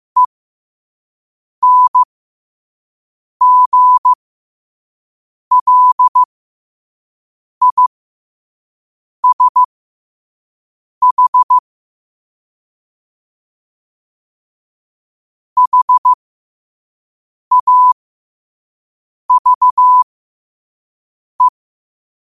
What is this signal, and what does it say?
Morse code, heard as a keyed tone, about 15 wpm, spaced as for 5 wpm, 1000 Hz. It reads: ENGLISH HAVE
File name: qotdfiles / 5wpm